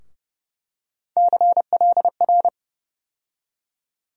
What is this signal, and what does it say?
Morse code, heard as a keyed tone, about 30 wpm, 700 Hz. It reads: CLR
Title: USA Call Sign Postfixes - 3 Letters 30wpm